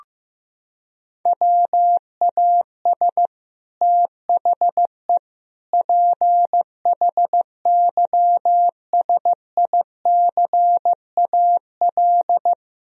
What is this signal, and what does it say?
Morse code, heard as a keyed tone, about 15 wpm, 700 Hz. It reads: WAS THE PHYSICAL
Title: Sentences from Top 1000 Words - Review Part 4 - 15wpm